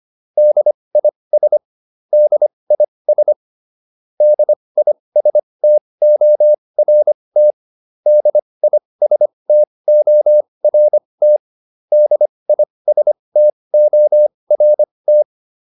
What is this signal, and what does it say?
Morse code, heard as a keyed tone, about 25 wpm, spaced as for 20 wpm, 600 Hz. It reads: DIS DIS DISTORT DISTORT DISTORT